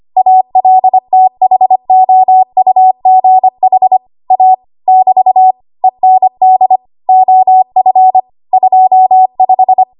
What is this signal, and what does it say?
Morse code, heard as a keyed tone, about 25 wpm, 750 Hz. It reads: ALTHOUGH A <BT> END OF 25
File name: Archive / 25 WPM / 25WPM